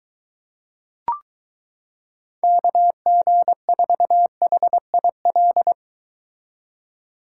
Morse code, heard as a keyed tone, about 22 wpm, 700 Hz. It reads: KG4HIL